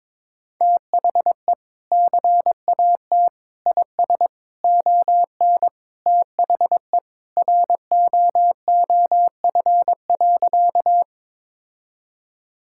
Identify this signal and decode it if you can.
Morse code, heard as a keyed tone, about 22 wpm, 700 Hz. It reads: THE CAT IS ON THE ROOF.
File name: practice / 22wpm / Sentences with 3-Word Phrases - 22wpm